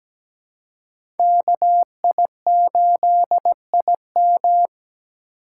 Morse code, heard as a keyed tone, about 17 wpm, 700 Hz. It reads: KI8IM
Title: USA Call Signs - 3x Character Spacing - 17wpm